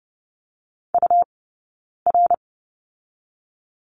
Morse code, heard as a keyed tone, about 30 wpm, spaced as for 9 wpm, 690 Hz. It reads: UR
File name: Lesson 18d - Letter m - Two Characters 30wpm